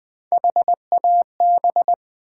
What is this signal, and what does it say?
Morse code, heard as a keyed tone, about 20 wpm, 700 Hz. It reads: HAB
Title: Build A Word - 1 Letter At A Time - 3 to 5-Letter Words - 20wpm